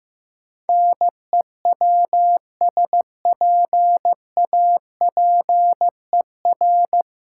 Morse code, heard as a keyed tone, about 15 wpm, 700 Hz. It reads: NEWSPAPER
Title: Sentences from Top 1100 Words - New Word Drill Alt - 15wpm